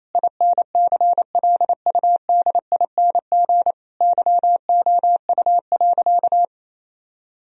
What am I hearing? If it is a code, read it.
Morse code, heard as a keyed tone, about 28 wpm, 700 Hz. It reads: INCLUDING YOU.
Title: Sentences from Top 1000 Words - Review Part 4 - 28wpm